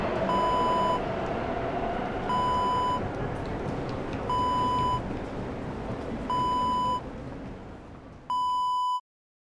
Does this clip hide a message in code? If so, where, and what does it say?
No.